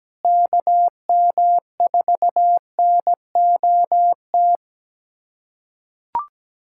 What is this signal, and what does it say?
Morse code, heard as a keyed tone, about 17 wpm, 700 Hz. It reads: KM4NOT
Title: USA Callsign DE Callsign - 6 Characters - 17wpm